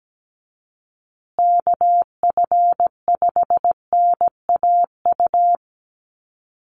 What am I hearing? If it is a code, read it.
Morse code, heard as a keyed tone, about 17 wpm, 700 Hz. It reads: KF5NAU
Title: USA Call Signs Repeated 2x 17wpm